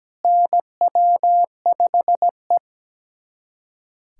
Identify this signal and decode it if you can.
Morse code, heard as a keyed tone, about 17 wpm, 700 Hz. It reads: NW5E
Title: USA Callsign DE Callsign - 4 Characters - 17wpm